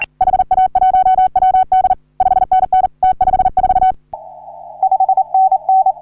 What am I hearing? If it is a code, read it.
Morse code, heard as a keyed tone, about 40 wpm, 750 Hz. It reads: HA1WD 5NN T54 5NN
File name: CW / HA3OU RHOBCW REC